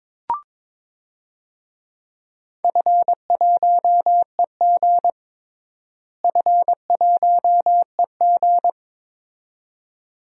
Morse code, heard as a keyed tone, about 22 wpm, 700 Hz. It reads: F1EG F1EG
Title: DXCC-100 Call Signs - 4 Characters - Repeated 2x - 22wpm